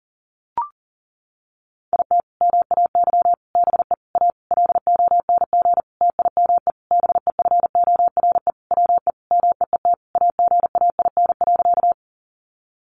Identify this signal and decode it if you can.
Morse code, heard as a keyed tone, about 40 wpm, 700 Hz. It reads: IT MAY BE A LONG TIME BEFORE WE MEET AGAIN.
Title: Sentences from Top 1100 Words 40wpm